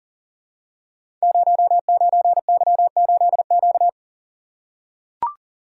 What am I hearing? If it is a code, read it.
Morse code, heard as a keyed tone, about 42 wpm, 700 Hz. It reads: 09Y8Q
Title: Code Groups - 5 Letter:Numbers - 40wpm